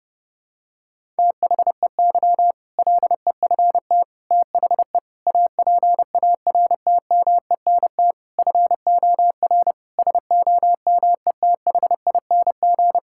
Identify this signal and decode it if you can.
Morse code, heard as a keyed tone, about 30 wpm, 700 Hz. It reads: THEY LEFT THE APARTMENT FOR SOMETHING
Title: Sentences from Top 1000 Words - Review Part 6 - 30wpm